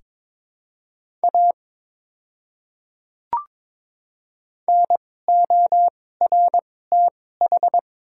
Morse code, heard as a keyed tone, about 22 wpm, spaced as for 16 wpm, 700 Hz. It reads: A NORTH
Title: 2x Character Spacing - Top 300 Words - 22wpm